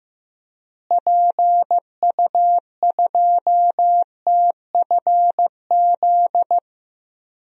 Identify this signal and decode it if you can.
Morse code, heard as a keyed tone, about 15 wpm, 700 Hz. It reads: PU2TFZ
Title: DXCC-100 Call Signs - 6 Characters - Repeated 2x - 15wpm